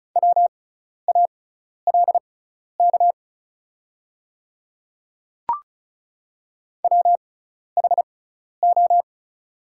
Morse code, heard as a keyed tone, about 34 wpm, spaced as for 12 wpm, 700 Hz. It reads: WALK WHO
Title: 6x Character Spacing - Top 500 Words - 35wpm